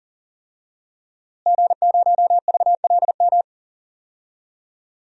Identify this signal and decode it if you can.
Morse code, heard as a keyed tone, about 40 wpm, 700 Hz. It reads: G0VLM
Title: DXCC-100 Call Signs - 5 Characters - Repeated 2x - 40wpm